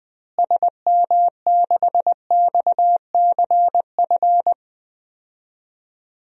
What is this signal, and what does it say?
Morse code, heard as a keyed tone, about 21 wpm, 700 Hz. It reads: SM6XCF
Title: DXCC-100 Call Signs - 6 Characters - Repeated 3x - 20wpm